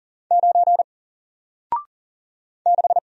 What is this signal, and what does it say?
Morse code, heard as a keyed tone, about 40 wpm, 700 Hz. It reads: 9 6
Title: Single Letter-Number Alt 40wpm